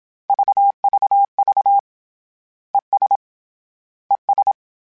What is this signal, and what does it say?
Morse code, heard as a keyed tone, about 27 wpm, 800 Hz. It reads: VVV ES ES
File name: General / General-mp3s / General @25